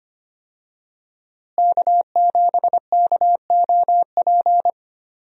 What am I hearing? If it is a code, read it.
Morse code, heard as a keyed tone, about 25 wpm, 700 Hz. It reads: K7KOP
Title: USA Call Signs - 5 Characters - Repeated 2x - 25wpm